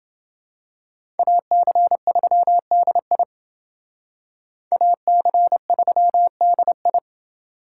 Morse code, heard as a keyed tone, about 28 wpm, 700 Hz. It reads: AC3DI AC3DI